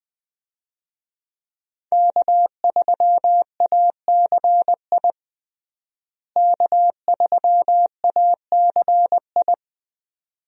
Morse code, heard as a keyed tone, about 20 wpm, 700 Hz. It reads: K3ACI K3ACI
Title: USA Call Signs - 5 Characters - Repeated 2x - 20wpm